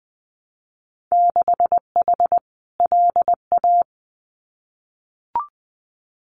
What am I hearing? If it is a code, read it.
Morse code, heard as a keyed tone, about 20 wpm, 700 Hz. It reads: 6H LA
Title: Winter Field Day Exchanges Encore 20wpm